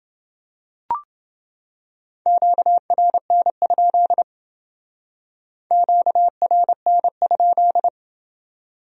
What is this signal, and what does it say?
Morse code, heard as a keyed tone, about 29 wpm, 700 Hz. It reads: QRN? QRN?